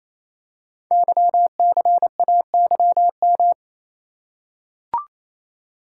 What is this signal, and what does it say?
Morse code, heard as a keyed tone, about 27 wpm, 700 Hz. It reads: YCAYM
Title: Code Groups - 5 Letters - 28wpm